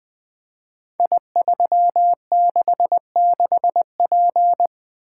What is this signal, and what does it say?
Morse code, heard as a keyed tone, about 21 wpm, 700 Hz. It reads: I366P
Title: Code Groups - 5 Letter:Numbers - 20wpm